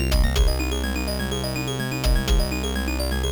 There is no code.